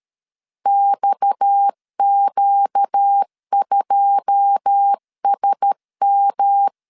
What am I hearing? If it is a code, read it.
Morse code, heard as a keyed tone, about 13 wpm, 800 Hz. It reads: XQ2SM
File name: LEZIONI / 10 SIGNS / 10C DOT 12wpm 2451